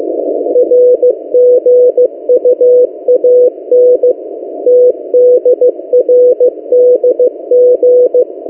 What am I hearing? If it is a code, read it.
Morse code, heard as a keyed tone, about 15 wpm, 500 Hz. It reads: RGUAN TDRDG
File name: M08a- -1355utc-8096khz